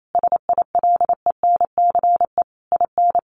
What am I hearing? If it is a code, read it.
Morse code, heard as a keyed tone, about 28 wpm, 700 Hz. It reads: SILENCE IN
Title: A Sound Of Thunder 28wpm